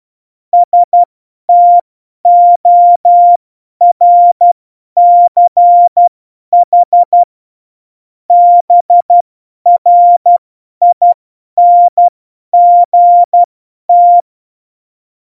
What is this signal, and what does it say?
Morse code, heard as a keyed tone, about 12 wpm, spaced as for 10 wpm, 700 Hz. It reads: STORCH BRINGT
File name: Morsekurs / Witze / Witz 03 April 2018, 50 F70